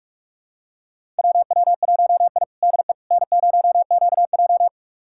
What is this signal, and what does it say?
Morse code, heard as a keyed tone, about 47 wpm, 700 Hz. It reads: WW1I DE N0QJ
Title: USA Callsign DE Callsign - 4 Characters - 45wpm